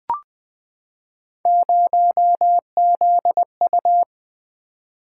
Morse code, heard as a keyed tone, about 20 wpm, 700 Hz. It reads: 0ZU
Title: Three Letter-Number Combinations - 20wpm